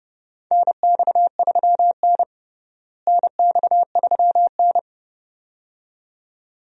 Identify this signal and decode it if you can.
Morse code, heard as a keyed tone, about 29 wpm, 700 Hz. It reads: NX3N NX3N
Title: USA Call Signs - 4 Characters - Repeated 2x - 30wpm